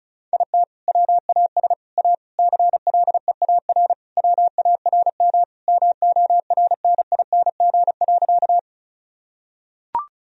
Morse code, heard as a keyed tone, about 36 wpm, 700 Hz. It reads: IT WAS A CLEAR WARM MORNING.
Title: Sentences from Top 1000 Words - Review Part 2 - 35wpm